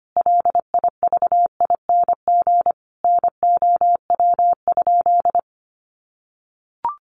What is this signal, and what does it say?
Morse code, heard as a keyed tone, about 25 wpm, 690 Hz. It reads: LIVING NOW?
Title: Sentences from Top 1100 Words 25wpm